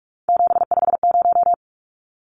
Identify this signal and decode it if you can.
Morse code, heard as a keyed tone, about 44 wpm, 700 Hz. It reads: 750
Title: Numbers - 3-Digit 45wpm